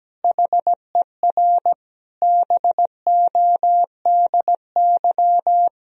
Morse code, heard as a keyed tone, about 17 wpm, 700 Hz. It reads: HER BODY
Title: Sentences from Top 1000 Words - Review Part 2 - 17wpm